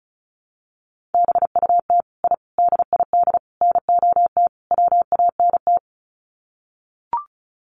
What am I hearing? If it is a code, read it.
Morse code, heard as a keyed tone, about 34 wpm, 700 Hz. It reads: BUT I DID NOT WANT
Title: Sets of 5 Words Encore 35wpm